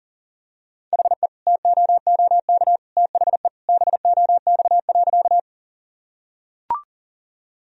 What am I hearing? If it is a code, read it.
Morse code, heard as a keyed tone, about 40 wpm, 700 Hz. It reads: HE TOOK THE BOX.
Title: Sentences from Top 1000 Words - Review Part 2 - 40wpm